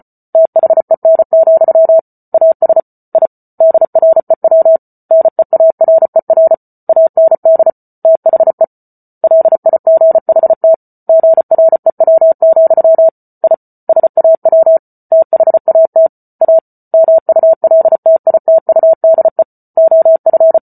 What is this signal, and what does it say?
Morse code, heard as a keyed tone, about 34 wpm, 650 Hz. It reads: THEN, AS I DREW NEARER AND THE LIGHT GREW, I SAW THAT A MULTITUDE OF